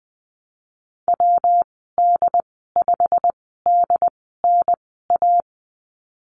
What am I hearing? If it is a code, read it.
Morse code, heard as a keyed tone, about 20 wpm, spaced as for 15 wpm, 700 Hz. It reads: WD5DNA
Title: 122 Lesson 20g - Number 3 - Call Signs 20wpm